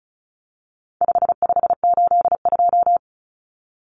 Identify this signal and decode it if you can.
Morse code, heard as a keyed tone, about 34 wpm, 700 Hz. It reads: 5582
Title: Numbers - 4-Digit 35wpm